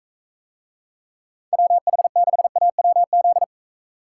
Morse code, heard as a keyed tone, about 45 wpm, 700 Hz. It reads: WH6AWZ